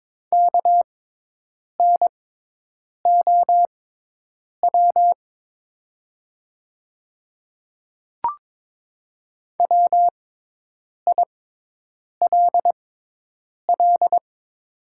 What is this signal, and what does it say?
Morse code, heard as a keyed tone, about 22 wpm, spaced as for 8 wpm, 700 Hz. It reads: KNOW WILL
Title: 6x Character Spacing - Top 100 Words - 22wpm